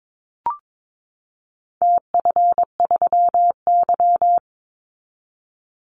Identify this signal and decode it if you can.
Morse code, heard as a keyed tone, about 22 wpm, 700 Hz. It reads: TF3Y
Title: Call Signs Repeated 3x - 22wpm